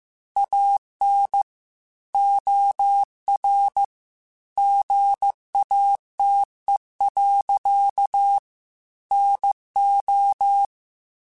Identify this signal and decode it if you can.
Morse code, heard as a keyed tone, about 15 wpm, 800 Hz. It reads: AN OR GATE. NO
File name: Technical Subjects / Technical Subjects-mp3s / OR Gate @13